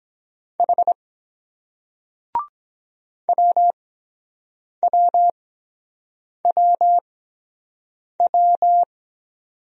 Morse code, heard as a keyed tone, about 23 wpm, spaced as for 7 wpm, 700 Hz. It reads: H WWWW